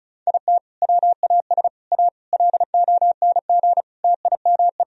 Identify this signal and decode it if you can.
Morse code, heard as a keyed tone, about 36 wpm, 700 Hz. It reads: IT WAS A LONG TIME
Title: Sentences from Top 1000 Words - Review Part 2 - 35wpm